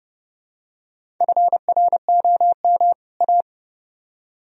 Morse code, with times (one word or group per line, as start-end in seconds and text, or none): 1.2-2.9 FROM
3.2-3.4 A